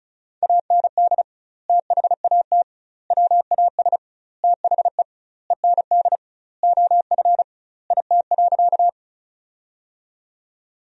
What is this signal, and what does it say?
Morse code, heard as a keyed tone, about 36 wpm, 700 Hz. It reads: AND THAT WAS THE END OF IT.
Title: 2x Word-Spacing - Sentences from Top 200 Words - 35wpm